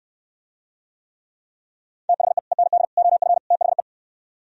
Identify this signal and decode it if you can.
Morse code, heard as a keyed tone, about 66 wpm, 700 Hz. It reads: THE END OF THE